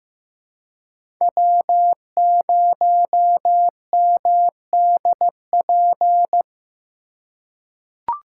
Morse code, heard as a keyed tone, about 15 wpm, 700 Hz. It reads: W0MDP
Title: USA Call Signs - 5 Characters - 4x Character Spacing - 15wpm